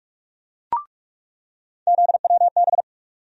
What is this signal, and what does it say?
Morse code, heard as a keyed tone, about 43 wpm, 700 Hz. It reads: ZWB
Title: USA Call Sign Postfixes - 3 Letters 45wpm